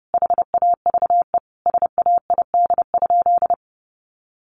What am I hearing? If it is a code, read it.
Morse code, heard as a keyed tone, about 30 wpm, 700 Hz. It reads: HAVE SAID?
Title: Sentences from Top 100 Words 30wpm